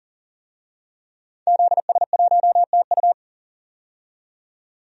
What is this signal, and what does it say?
Morse code, heard as a keyed tone, about 40 wpm, 700 Hz. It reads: ZS1TU
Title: DXCC-100 Call Signs - 5 Characters - 40wpm